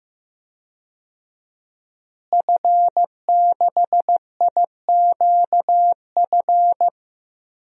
Morse code, heard as a keyed tone, about 15 wpm, 700 Hz. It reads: F6IQF
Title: DXCC-100 Call Signs - 5 Characters - Repeated 2x - 15wpm